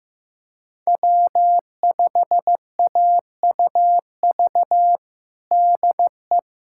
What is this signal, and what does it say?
Morse code, heard as a keyed tone, about 15 wpm, 700 Hz. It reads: W5AUV DE
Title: USA Callsign DE Callsign 15wpm